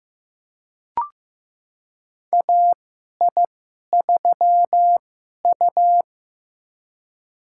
Morse code, heard as a keyed tone, about 15 wpm, spaced as for 11 wpm, 700 Hz. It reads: AI3U
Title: USA Call Signs - 2x Character Spacing - 15wpm